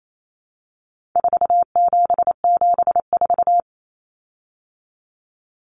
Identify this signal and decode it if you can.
Morse code, heard as a keyed tone, about 28 wpm, 700 Hz. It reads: 4774
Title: Common Cut Numbers - 4-Digit 28wpm